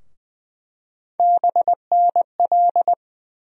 Morse code, heard as a keyed tone, about 20 wpm, 700 Hz. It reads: BNL
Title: USA Call Sign Postfixes - 3 Letters 20wpm